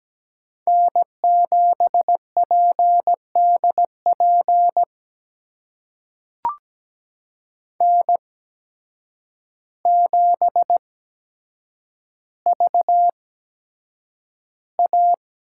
Morse code, heard as a keyed tone, about 17 wpm, spaced as for 8 wpm, 700 Hz. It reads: N7PDP N7VA
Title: USA Call Signs - 5 Characters - 8x Character Spacing - 17wpm